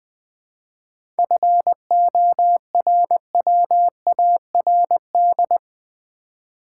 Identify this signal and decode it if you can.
Morse code, heard as a keyed tone, about 20 wpm, 700 Hz. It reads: FORWARD
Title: 7 Letter Words - First 3 Letters Given - 20wpm